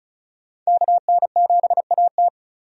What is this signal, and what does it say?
Morse code, heard as a keyed tone, about 34 wpm, 700 Hz. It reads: KN7AT
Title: USA Call Signs - 5 Characters - 35wpm